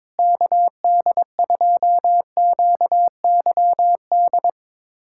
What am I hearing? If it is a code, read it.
Morse code, heard as a keyed tone, about 21 wpm, 700 Hz. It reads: KD2QYD
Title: USA Callsign DE Callsign - 6 Characters - 22wpm